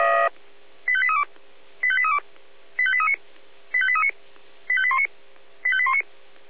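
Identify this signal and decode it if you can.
Five-tone selective call (ZVEI), heard as quick runs of five tones, tones of about 70 ms. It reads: 85832 85832 85839 85839 85819 85819
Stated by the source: ZVEI five-tone selective call received by SDR.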